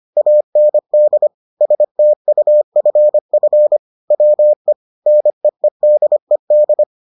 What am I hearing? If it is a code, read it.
Morse code, heard as a keyed tone, about 25 wpm, 600 Hz. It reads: AND STUFF WE NEEDED